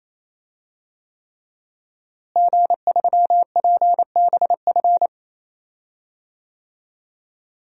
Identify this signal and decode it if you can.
Morse code, heard as a keyed tone, about 28 wpm, 700 Hz. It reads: G3PBF
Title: DXCC-100 Call Signs - 5 Characters - Repeated 3x - 28wpm